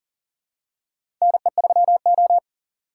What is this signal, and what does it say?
Morse code, heard as a keyed tone, about 37 wpm, 700 Hz. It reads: NE3O
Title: USA Call Signs - 4 Characters - 4x Character Spacing - 40wpm